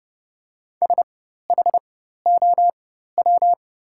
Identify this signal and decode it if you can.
Morse code, heard as a keyed tone, about 30 wpm, spaced as for 14 wpm, 700 Hz. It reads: SHOW
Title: 4x Character Spacing - Top 200 Words - 30wpm